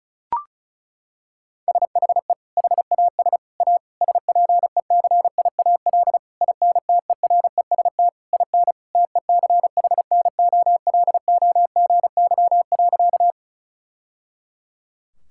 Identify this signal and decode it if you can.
Morse code, heard as a keyed tone, about 36 wpm, 700 Hz. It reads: SHE HAS A SPECIAL INTEREST IN TECHNOLOGY.
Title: Sentences from Top 1000 Words - Review Part 5 - 35wpm